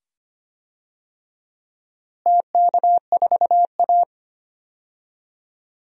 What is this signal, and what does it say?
Morse code, heard as a keyed tone, about 25 wpm, 700 Hz. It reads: TK4A
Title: DXCC-100 Call Signs - 4 Characters - Repeated 2x - 25wpm